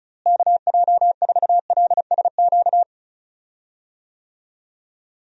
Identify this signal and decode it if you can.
Morse code, heard as a keyed tone, about 34 wpm, 700 Hz. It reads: KJ4LSQ
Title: USA Call Signs - 6 Characters - Repeated 2x - 35wpm